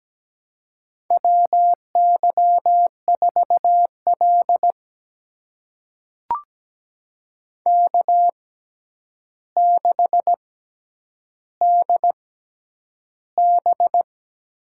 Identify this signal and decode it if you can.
Morse code, heard as a keyed tone, about 17 wpm, spaced as for 9 wpm, 700 Hz. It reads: WY4L K6DB
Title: USA Call Signs - 4 Characters - 6x Character Spacing - 17wpm